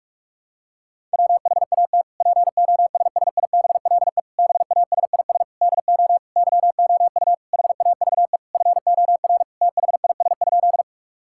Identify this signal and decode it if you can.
Morse code, heard as a keyed tone, about 46 wpm, 700 Hz. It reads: WHAT POSSIBLE BASIS DO YOU HAVE FOR THIS?
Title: Sentences from Top 1000 Words - Review Part 5 - 45wpm